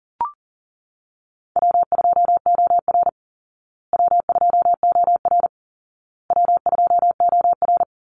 Morse code, heard as a keyed tone, about 40 wpm, 700 Hz. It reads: W2OR W2OR W2OR
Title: Easy Call Signs Repeated 3x - 40wpm